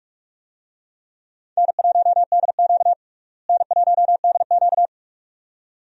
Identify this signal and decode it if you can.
Morse code, heard as a keyed tone, about 41 wpm, 700 Hz. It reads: N1DQ N1DQ
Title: USA Call Signs - 4 Characters - Repeated 2x - 45wpm